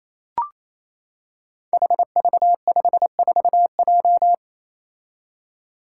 Morse code, heard as a keyed tone, about 28 wpm, 700 Hz. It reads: HV54J
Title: Five Challenging Letter-Number Combinations 28wpm